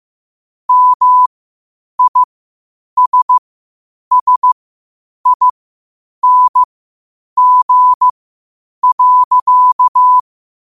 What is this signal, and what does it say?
Morse code, heard as a keyed tone, about 15 wpm, spaced as for 8 wpm, 1000 Hz. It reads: MISSING.
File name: qotdfiles / 8wpm